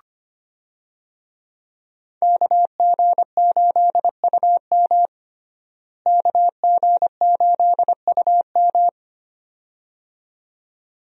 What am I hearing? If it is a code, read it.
Morse code, heard as a keyed tone, about 25 wpm, 700 Hz. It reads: KG8UM KG8UM
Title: USA Call Signs - 5 Characters - Repeated 2x - 25wpm